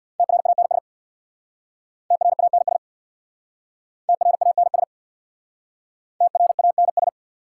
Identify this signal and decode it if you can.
Morse code, heard as a keyed tone, about 66 wpm, 700 Hz. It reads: TRANS TRANS TRANS TRANS